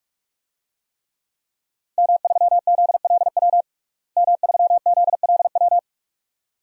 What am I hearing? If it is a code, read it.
Morse code, heard as a keyed tone, about 46 wpm, 700 Hz. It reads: M3ZLW M3ZLW
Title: DXCC-100 Call Signs - 5 Characters - Repeated 2x - 45wpm